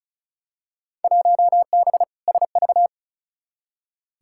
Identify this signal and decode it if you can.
Morse code, heard as a keyed tone, about 34 wpm, 700 Hz. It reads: 1B SV
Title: Field Day Exchanges Encore - 35wpm